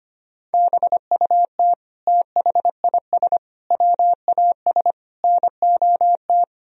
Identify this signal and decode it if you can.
Morse code, heard as a keyed tone, about 25 wpm, 700 Hz. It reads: BUT THIS WAS NOT
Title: Sentences from Top 1000 Words - Review Part 3 - 25wpm